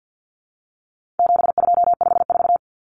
Morse code, heard as a keyed tone, about 48 wpm, 700 Hz. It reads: 7254
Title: Common Cut Numbers - 4-Digit 50wpm